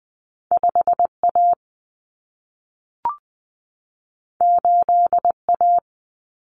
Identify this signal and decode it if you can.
Morse code, heard as a keyed tone, about 20 wpm, 700 Hz. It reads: 5A 8A
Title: Common Cut Numbers - 2-Digit 20wpm